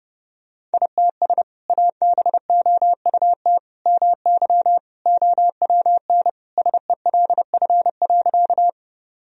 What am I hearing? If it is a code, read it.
Morse code, heard as a keyed tone, about 30 wpm, 700 Hz. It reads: ITS ABOUT MY OWN SELF.
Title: Sentences from Top 1000 Words - Review Part 1 - 30wpm